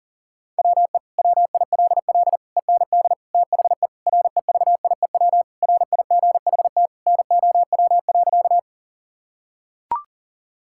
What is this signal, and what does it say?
Morse code, heard as a keyed tone, about 40 wpm, 700 Hz. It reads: WE WILL END THE REVIEW RIGHT NOW.
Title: Sentences from Top 1000 Words - Review Part 6 - 40wpm